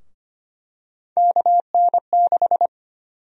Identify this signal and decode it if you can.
Morse code, heard as a keyed tone, about 25 wpm, 700 Hz. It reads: KN6
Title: USA Call Sign Prefixes - Repeated 2x - 25wpm